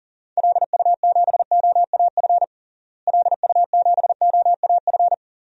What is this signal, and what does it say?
Morse code, heard as a keyed tone, about 40 wpm, 700 Hz. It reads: LU7OAF LU7OAF
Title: DXCC-100 Call Signs - 6 Characters - Repeated 2x - 40wpm